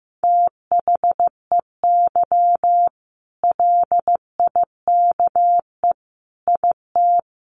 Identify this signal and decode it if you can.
Morse code, heard as a keyed tone, about 14 wpm, 700 Hz. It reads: THEY LIKE IT
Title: Sentences from Top 600 Words 15wpm